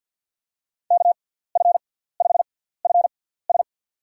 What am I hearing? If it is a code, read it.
Morse code, heard as a keyed tone, about 51 wpm, spaced as for 17 wpm, 700 Hz. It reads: KF5FS
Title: USA Call Signs - 5 Characters - 6x Character Spacing - 50wpm